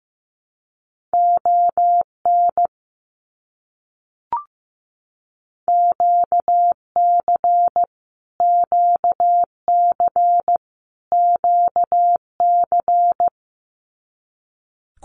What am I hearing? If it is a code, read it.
Morse code, heard as a keyed tone, about 15 wpm, 700 Hz. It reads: ON QC QC QC
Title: Canadian Provinces and Territories - Repeated 3x - 15wpm